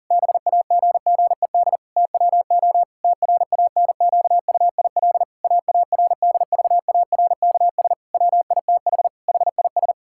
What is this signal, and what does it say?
Morse code, heard as a keyed tone, about 40 wpm, 700 Hz. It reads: BAGGED TWO TRANQUIL AARDVARKS WITH HIS
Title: Pangram Challenge v2 - 40wpm